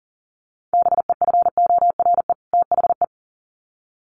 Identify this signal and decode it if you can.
Morse code, heard as a keyed tone, about 40 wpm, 700 Hz. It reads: BEFORE THE
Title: Sets of 2 Words 40wpm